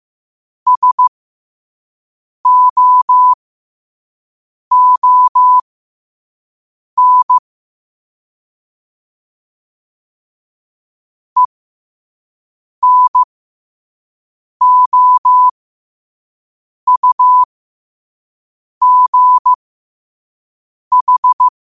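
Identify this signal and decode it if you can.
Morse code, heard as a keyed tone, about 15 wpm, spaced as for 5 wpm, 1000 Hz. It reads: SOON ENOUGH
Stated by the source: Morse quote-of-the-day practice at 5 wpm.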